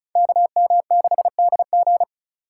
Morse code, heard as a keyed tone, about 34 wpm, 700 Hz. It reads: KM6DG
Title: USA Call Signs - 5 Characters - 2x Character Spacing - 35wpm